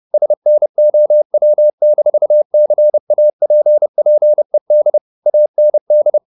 Morse code, heard as a keyed tone, about 30 wpm, 600 Hz. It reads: SNOW-CAPPED AND
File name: the old man and the sea / the old man and the sea 1 30wpm